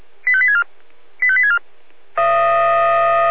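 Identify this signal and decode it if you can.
Five-tone selective call (ZVEI), heard as quick runs of five tones, tones of about 70 ms. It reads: 85864 85864
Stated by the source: ZVEI five-tone selective call received on an SDR.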